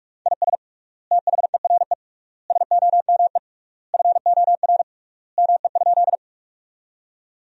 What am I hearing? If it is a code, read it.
Morse code, heard as a keyed tone, about 46 wpm, 700 Hz. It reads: IS THERE SOME FOR ME?
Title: 3x Word-Spacing - Sentences from Top 200 Words - 45wpm